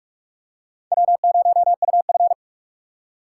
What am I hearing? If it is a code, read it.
Morse code, heard as a keyed tone, about 46 wpm, 700 Hz. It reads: W0UF
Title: K1USN SST QSO Practice - 45wpm